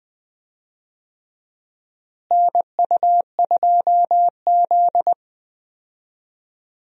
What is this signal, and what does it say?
Morse code, heard as a keyed tone, about 21 wpm, 700 Hz. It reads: NU2Z
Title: USA Call Signs - 4 Characters - Repeated 3x - 20wpm